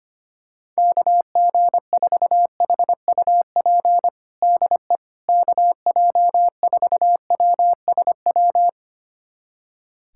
Morse code, heard as a keyed tone, about 25 wpm, 700 Hz. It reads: KG4HUP DE KJ4WSW
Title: USA Callsign DE Callsign - 6 Characters - 25wpm